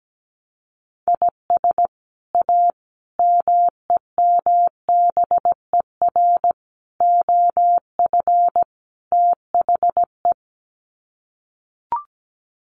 Morse code, heard as a keyed tone, about 17 wpm, 700 Hz. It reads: IS A MEMBER OF THE
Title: Sets of 5 Words 17wpm